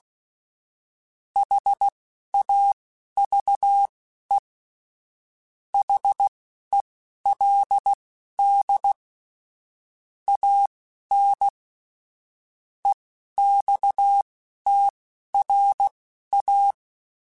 Morse code, heard as a keyed tone, about 16 wpm, spaced as for 11 wpm, 800 Hz. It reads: HAVE HELD AN EXTRA
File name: QSOs / QSOs-mp3s / kc7sac@10